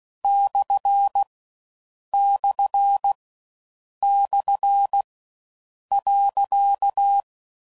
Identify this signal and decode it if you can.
Morse code, heard as a keyed tone, about 16 wpm, 800 Hz. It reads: / / / .
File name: Mixed / Misc / Punct